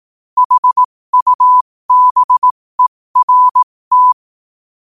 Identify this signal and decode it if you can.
Morse code, heard as a keyed tone, about 18 wpm, spaced as for 15 wpm, 1000 Hz. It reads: HUBERT
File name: qotdfiles / 15wpm